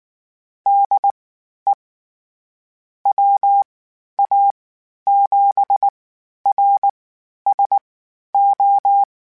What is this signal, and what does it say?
Morse code, heard as a keyed tone, about 19 wpm, spaced as for 11 wpm, 800 Hz. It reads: DE WA7RSO
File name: QST QST @18